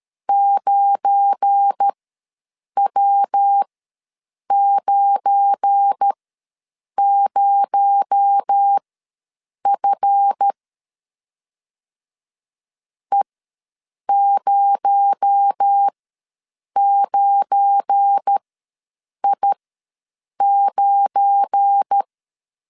Morse code, heard as a keyed tone, about 13 wpm, spaced as for 7 wpm, 800 Hz. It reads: 9W90F E09I9
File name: LEZIONI / 7 GF9Z / 7 GF9 ws 12wpm 3344